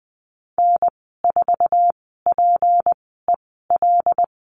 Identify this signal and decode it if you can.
Morse code, heard as a keyed tone, about 20 wpm, spaced as for 15 wpm, 700 Hz. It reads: N4PEL
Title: 173 Lesson 28g - Number 9 - Call Signs 20wpm